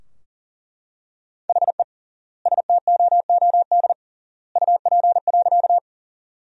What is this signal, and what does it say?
Morse code, heard as a keyed tone, about 40 wpm, 700 Hz. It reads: HE STOOD UP.
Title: 3x Word-Spacing - Sentences from Top 500 Words - 40wpm